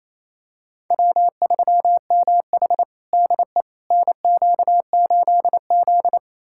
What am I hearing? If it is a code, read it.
Morse code, heard as a keyed tone, about 28 wpm, 700 Hz. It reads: W3MH DE NQ8Z
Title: USA Callsign DE Callsign - 4 Characters - 28wpm